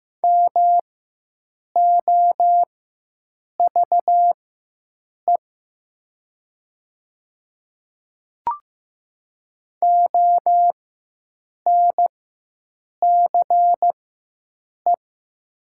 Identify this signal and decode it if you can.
Morse code, heard as a keyed tone, about 15 wpm, spaced as for 7 wpm, 700 Hz. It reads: MOVE ONCE
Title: 4x Character Spacing - Top 500 Words - 15wpm